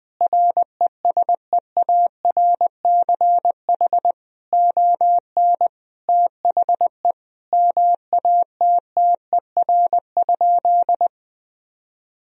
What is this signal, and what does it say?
Morse code, heard as a keyed tone, about 20 wpm, 700 Hz. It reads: RESEARCH ON THE MATTER?
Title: Sentences from Top 1000 Words - Review Part 4 - 20wpm